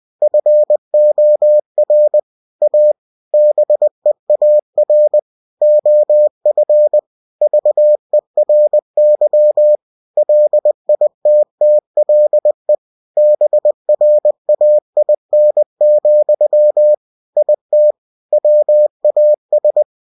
Morse code, heard as a keyed tone, about 20 wpm, 600 Hz. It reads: FOR A BEAR OF VERY LITTLE BRAIN, IT WAS